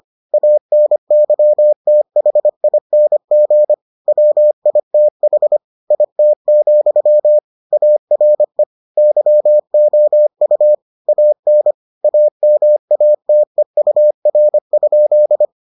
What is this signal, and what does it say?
Morse code, heard as a keyed tone, about 25 wpm, 600 Hz. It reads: ANYTHING WITH IT, ARE YOU AN AMATEUR?